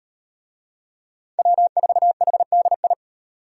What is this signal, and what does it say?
Morse code, heard as a keyed tone, about 36 wpm, 700 Hz. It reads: W4HDI